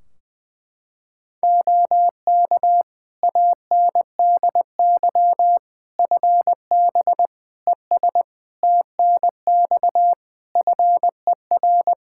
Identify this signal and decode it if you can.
Morse code, heard as a keyed tone, about 21 wpm, 700 Hz. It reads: OK ANDY FB ES TNX FER
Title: Practice QSO - Exchange 2 - Info - 20wpm